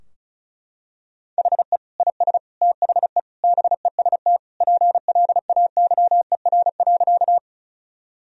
Morse code, heard as a keyed tone, about 36 wpm, 700 Hz. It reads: HE IS THE BEST PLAYER.
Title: Sentences from Top 1000 Words - Review Part 4 - 35wpm